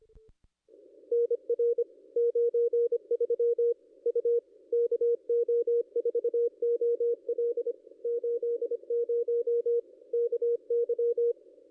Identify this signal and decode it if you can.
Morse code, heard as a keyed tone, about 25 wpm, 470 Hz. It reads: NR 93 U KO4OL 80 KY
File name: contests / ss-cw-k5tr-op-n5rz -0400